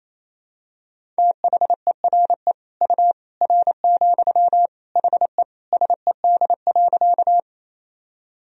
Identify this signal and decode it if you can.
Morse code, heard as a keyed tone, about 28 wpm, 700 Hz. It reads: THERE U R, HE SED.
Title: Sentences from Top 100 Words and Abbreviations 28wpm